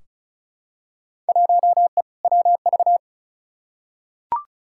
Morse code, heard as a keyed tone, about 34 wpm, 700 Hz. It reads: 1E WV
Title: Field Day Exchanges Encore - 35wpm